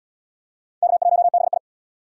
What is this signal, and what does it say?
Morse code, heard as a keyed tone, about 70 wpm, 700 Hz. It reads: K1BI